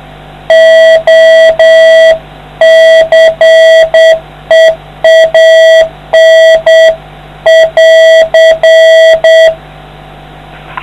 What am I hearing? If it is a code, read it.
Morse code, heard as a keyed tone, about 8 wpm, 650 Hz. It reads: OCEAN <AR>